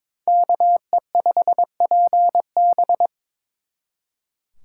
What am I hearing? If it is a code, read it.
Morse code, heard as a keyed tone, about 26 wpm, 700 Hz. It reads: KE5PB
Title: Speed Racing - USA 2x2 Call Signs 22wpm